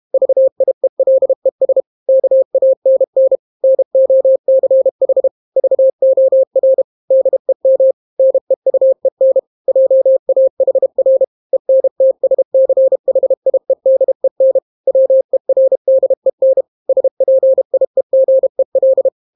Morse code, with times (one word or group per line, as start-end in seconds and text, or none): 0.1-1.8 VIELES
2.1-3.4 KANN
3.6-5.3 NOCH
5.6-6.8 VOR
7.1-7.9 DEM
8.2-9.4 NEUEN
9.7-11.2 JAHR
11.5-14.6 ENTSCHIEDEN
14.9-16.6 WERDEN
16.9-19.1 SPIEGEL